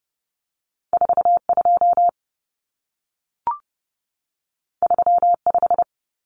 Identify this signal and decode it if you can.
Morse code, heard as a keyed tone, about 30 wpm, 700 Hz. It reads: 42 35
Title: Numbers - 2-Digit 30wpm